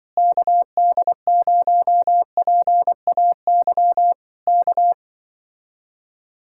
Morse code, heard as a keyed tone, about 24 wpm, 700 Hz. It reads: KD0PAY K
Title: Speed-Racing Common QSO Phrases - 20wpm